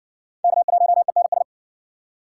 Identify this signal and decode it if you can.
Morse code, heard as a keyed tone, about 50 wpm, 700 Hz. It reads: K1ENS
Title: Speed Racing - USA 1x3 Call Signs 40wpm